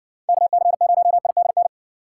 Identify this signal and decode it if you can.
Morse code, heard as a keyed tone, about 60 wpm, 700 Hz. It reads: KC0IDN